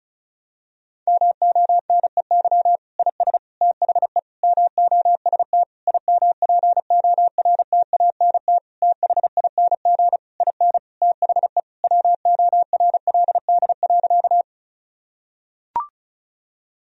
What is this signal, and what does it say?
Morse code, heard as a keyed tone, about 34 wpm, 700 Hz. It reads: MONEY IS THE MOST IMPORTANT THING IN THE WORLD.
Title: Sentences from Top 1000 Words - Review Part 1 - 35wpm